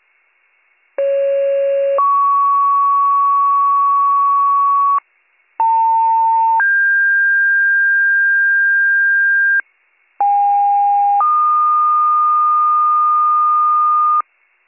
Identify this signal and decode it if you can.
Two-tone page, heard as a one-second tone, then a three-second tone, about 550 then 1120 Hz, about 900 then 1600 Hz, about 810 then 1220 Hz. It is heard from 1.0 to 14.2 s.